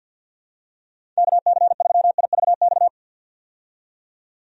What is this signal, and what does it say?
Morse code, heard as a keyed tone, about 53 wpm, 700 Hz. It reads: KC3IVX
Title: USA Call Signs - 6 Characters - 50wpm